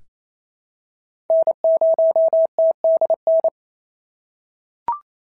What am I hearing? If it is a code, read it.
Morse code, heard as a keyed tone, about 25 wpm, 650 Hz. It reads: N0TDN
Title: SOTA QSO Practice 28wpm